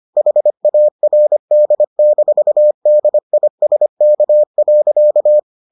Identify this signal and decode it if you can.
Morse code, heard as a keyed tone, about 25 wpm, 600 Hz. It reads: HARD-DISK.